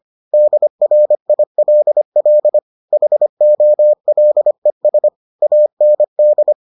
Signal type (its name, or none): Morse code